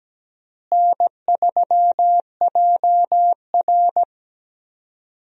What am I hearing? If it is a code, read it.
Morse code, heard as a keyed tone, about 17 wpm, 700 Hz. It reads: N3JR
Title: USA Call Signs - 4 Characters - Repeated 3x - 17wpm